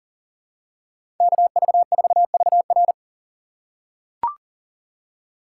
Morse code, heard as a keyed tone, about 40 wpm, 700 Hz. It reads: KV4VR
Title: USA Call Signs - 5 Characters - 2x Character Spacing - 40wpm